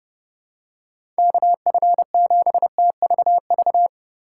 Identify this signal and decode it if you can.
Morse code, heard as a keyed tone, about 30 wpm, 700 Hz. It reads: KF7TVV
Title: USA Call Signs - 6 Characters - 8x Character Spacing - 30wpm